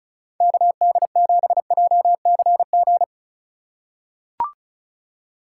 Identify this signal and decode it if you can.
Morse code, heard as a keyed tone, about 34 wpm, 700 Hz. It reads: KD7JCG
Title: USA Call Signs - 6 Characters - 2x Character Spacing - 35wpm